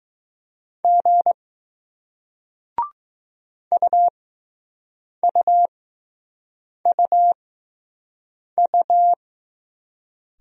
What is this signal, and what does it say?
Morse code, heard as a keyed tone, about 20 wpm, spaced as for 6 wpm, 700 Hz. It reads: G UUUU